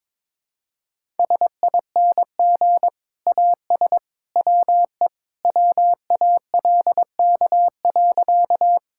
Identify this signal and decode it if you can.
Morse code, heard as a keyed tone, about 22 wpm, 700 Hz. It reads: SING AS WE WALK.
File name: practice / 22wpm / Sentences with 2-Word Phrases - 22wpm